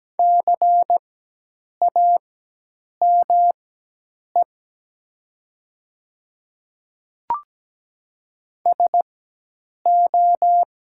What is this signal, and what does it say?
Morse code, heard as a keyed tone, about 17 wpm, spaced as for 8 wpm, 700 Hz. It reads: CAME SO
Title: 4x Character Spacing - Top 200 Words - 17wpm